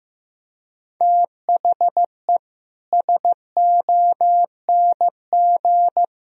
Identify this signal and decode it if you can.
Morse code, heard as a keyed tone, about 16 wpm, 700 Hz. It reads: THE SONG
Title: Sentences from Top 1000 Words - Review Part 3 - 15wpm